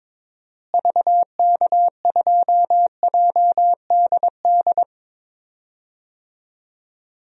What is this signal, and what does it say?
Morse code, heard as a keyed tone, about 22 wpm, 700 Hz. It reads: VK2JDD
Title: DXCC-100 Call Signs - 6 Characters - Repeated 2x - 22wpm